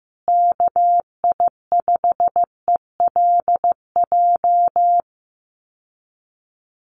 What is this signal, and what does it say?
Morse code, heard as a keyed tone, about 15 wpm, 700 Hz. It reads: KI5ELJ
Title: USA Call Signs Repeated 2x 15wpm